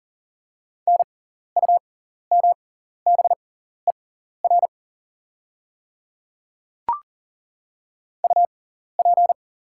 Morse code, heard as a keyed tone, about 40 wpm, spaced as for 14 wpm, 700 Hz. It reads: NUMBER UP
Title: 6x Character Spacing - Top 100 Words - 40wpm